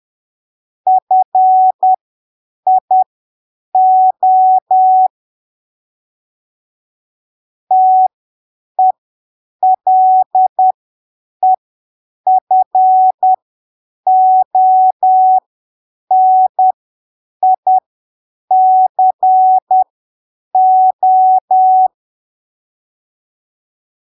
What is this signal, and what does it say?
Morse code, heard as a keyed tone, about 10 wpm, spaced as for 7 wpm, 750 Hz. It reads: FIO TELEFONICO
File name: arquivos / texto / texto 13 10ppm